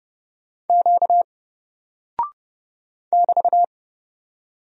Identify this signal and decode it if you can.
Morse code, heard as a keyed tone, about 29 wpm, 700 Hz. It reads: Q <BT>
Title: Single Character (Letter:Number:Punctuation:Prosign) 30wpm